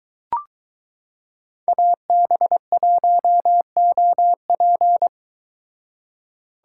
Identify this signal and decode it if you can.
Morse code, heard as a keyed tone, about 22 wpm, 700 Hz. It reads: AB1OP